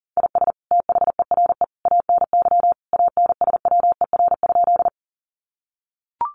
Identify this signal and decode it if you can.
Morse code, heard as a keyed tone, about 40 wpm, 700 Hz. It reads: IS THERE ANY ANSWER?
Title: Sentences from Top 300 Words 40wpm